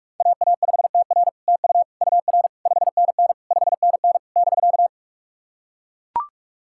Morse code, heard as a keyed tone, about 44 wpm, 700 Hz. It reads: AA5TR TU UR 5NN 5NN <BK>
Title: SOTA QSO Practice 45wpm